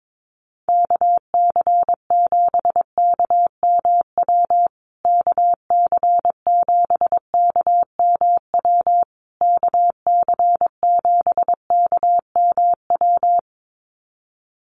Morse code, heard as a keyed tone, about 22 wpm, 700 Hz. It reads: KC7KMW KC7KMW KC7KMW
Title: USA Call Signs Repeated 3x 22wpm